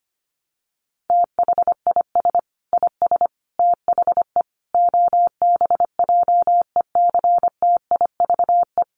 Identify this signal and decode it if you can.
Morse code, heard as a keyed tone, about 25 wpm, 700 Hz. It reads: THIS IS THE OBJECTIVE
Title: Sentences from Top 1200 Words 25wpm